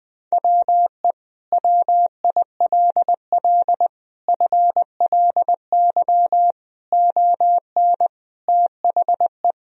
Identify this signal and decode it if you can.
Morse code, heard as a keyed tone, about 20 wpm, 700 Hz. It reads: WE WILL FLY ON THE
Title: Sentences from Top 1000 Words - Review Part 3 - 20wpm